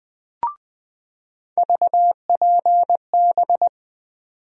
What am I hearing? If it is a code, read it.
Morse code, heard as a keyed tone, about 20 wpm, 700 Hz. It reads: VPB
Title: USA Call Sign Postfixes - 3 Letters 20wpm